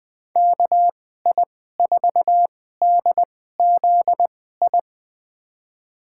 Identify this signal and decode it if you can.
Morse code, heard as a keyed tone, about 20 wpm, spaced as for 15 wpm, 700 Hz. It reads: KI4DZI
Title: USA Call Signs - 6 Characters - 2x Character Spacing - 20wpm